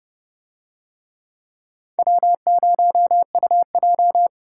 Morse code, heard as a keyed tone, about 28 wpm, 700 Hz. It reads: W0UJ